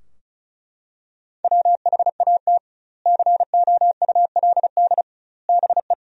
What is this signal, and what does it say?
Morse code, heard as a keyed tone, about 36 wpm, 700 Hz. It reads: WHAT COULD BE
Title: 2x Word-Spacing - Sentences from Top 400 Words - 35wpm